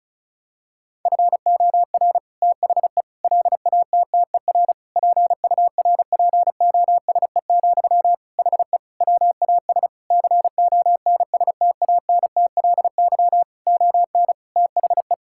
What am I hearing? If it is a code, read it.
Morse code, heard as a keyed tone, about 34 wpm, 700 Hz. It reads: FOR THE LATTER PURPOSE, HE WAS CONSTANTLY ON THE
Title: Sentences from Top 1600 Words 35wpm